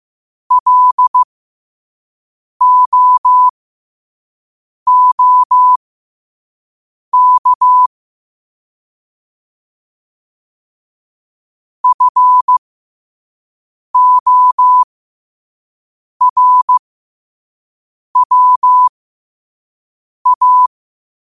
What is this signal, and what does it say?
Morse code, heard as a keyed tone, about 15 wpm, spaced as for 5 wpm, 1000 Hz. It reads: LOOK FORWA
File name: qotdfiles / 5wpm